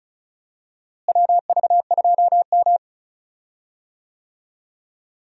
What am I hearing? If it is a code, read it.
Morse code, heard as a keyed tone, about 36 wpm, 700 Hz. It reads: WV2M
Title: USA Call Signs - 4 Characters - Repeated 2x - 35wpm